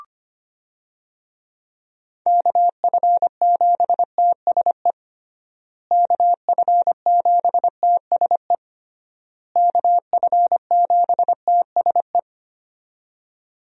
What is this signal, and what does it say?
Morse code, heard as a keyed tone, about 25 wpm, 700 Hz. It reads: KF7TSE KF7TSE KF7TSE
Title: USA Call Signs - 6 Characters - Repeated 3x - 25wpm